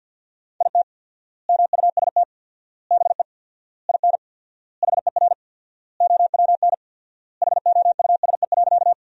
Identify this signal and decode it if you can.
Morse code, heard as a keyed tone, about 50 wpm, 700 Hz. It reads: IT MUST BE IN HER OWN HOUSE.
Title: 4x Word-Spacing - Sentences from Top 300 Words - 50wpm